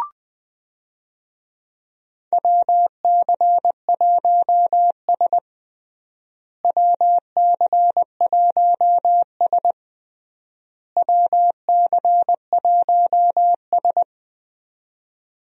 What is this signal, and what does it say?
Morse code, heard as a keyed tone, about 20 wpm, 700 Hz. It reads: WC1S WC1S WC1S
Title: USA Call Signs - 4 Characters - Repeated 3x - 20wpm